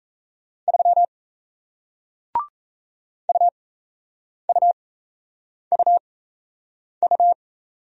Morse code, heard as a keyed tone, about 38 wpm, spaced as for 8 wpm, 700 Hz. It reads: 3 UUUU